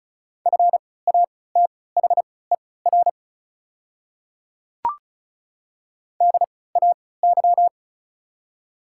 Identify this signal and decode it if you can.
Morse code, heard as a keyed tone, about 36 wpm, spaced as for 20 wpm, 690 Hz. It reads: FATHER DAY
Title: 3x Character Spacing - Top 300 Words - 35wpm